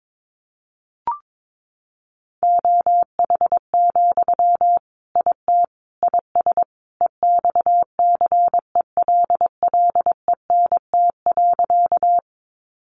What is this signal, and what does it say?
Morse code, heard as a keyed tone, about 22 wpm, 700 Hz. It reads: OH, IT IS EXCELLENT.
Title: Sentences from Top 1000 Words 22wpm